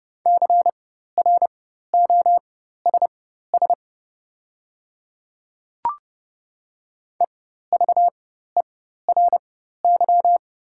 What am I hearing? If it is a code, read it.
Morse code, heard as a keyed tone, about 30 wpm, spaced as for 14 wpm, 700 Hz. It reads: CROSS EVERY
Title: 4x Character Spacing - Top 300 Words - 30wpm